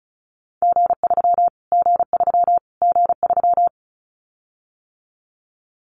Morse code, heard as a keyed tone, about 34 wpm, 700 Hz. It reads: G3 G3 G3
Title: Callsign Prefixes for Top 25 DXCC - Repeated 3x - 35wpm